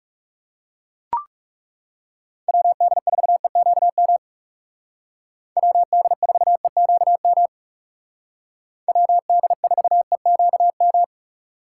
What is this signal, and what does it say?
Morse code, heard as a keyed tone, about 36 wpm, 700 Hz. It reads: WD4EQM WD4EQM WD4EQM
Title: Speed Racing - USA 2x3 Call Signs 30wpm